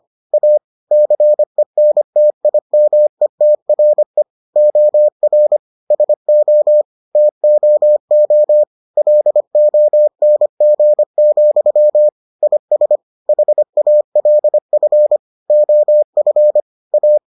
Morse code, heard as a keyed tone, about 25 wpm, 600 Hz. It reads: A CENTIMETRE OR SO TOO LONG, IS HALF OF A